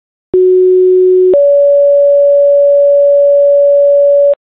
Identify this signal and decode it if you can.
Two-tone page, heard as a one-second tone, then a three-second tone, about 370 then 570 Hz, with noise. It is heard from 0.3 to 4.3 s.